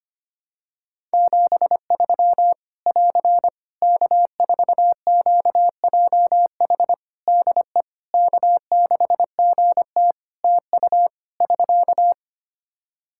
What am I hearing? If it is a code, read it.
Morse code, heard as a keyed tone, about 25 wpm, 700 Hz. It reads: 73 <AR> K4QJH DE K6GT TU <SK>
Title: Practice QSO - Exchange 4 - Ending - 25wpm